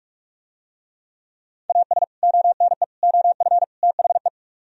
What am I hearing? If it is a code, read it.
Morse code, heard as a keyed tone, about 45 wpm, 700 Hz. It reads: AS ONE OF THE